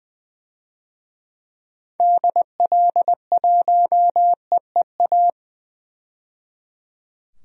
Morse code, heard as a keyed tone, about 20 wpm, 700 Hz. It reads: DL1EEA
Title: DXCC-100 Call Signs - 6 Characters - 20wpm